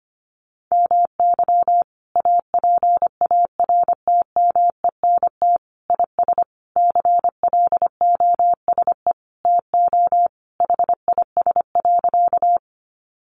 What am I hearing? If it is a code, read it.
Morse code, heard as a keyed tone, about 25 wpm, 700 Hz. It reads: MY APARTMENT IS CLOSE TO HIS.
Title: Sentences from Top 900 Words 25wpm